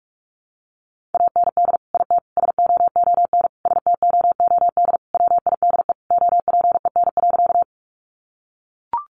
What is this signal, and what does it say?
Morse code, heard as a keyed tone, about 44 wpm, 700 Hz. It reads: AND IT SOON STOOD WIDE OPEN.
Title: Sentences from Top 900 Words 45wpm